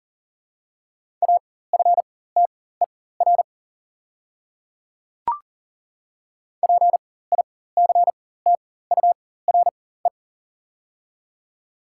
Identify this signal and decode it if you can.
Morse code, heard as a keyed tone, about 37 wpm, spaced as for 18 wpm, 700 Hz. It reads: AFTER PICTURE
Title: 4x Character Spacing - Top 300 Words - 40wpm